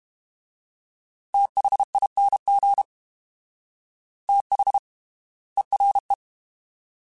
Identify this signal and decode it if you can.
Morse code, heard as a keyed tone, about 32 wpm, spaced as for 25 wpm, 800 Hz. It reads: THING THERE
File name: TH / TH mp3s / Fast TH @30